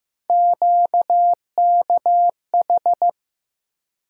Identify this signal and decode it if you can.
Morse code, heard as a keyed tone, about 15 wpm, 700 Hz. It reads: QKH
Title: USA Call Sign Postfixes - 3 Letters 15wpm